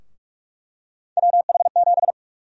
Morse code, heard as a keyed tone, about 46 wpm, 700 Hz. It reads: WH7
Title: USA Call Sign Prefixes - 45wpm